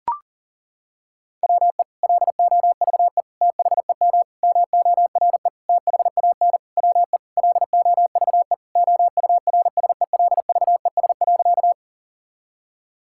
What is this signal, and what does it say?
Morse code, heard as a keyed tone, about 40 wpm, 700 Hz. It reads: WE LOVE THEM MORE THAN WE LOVE OURSELVES.
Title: Sentences from Top 1600 Words 40wpm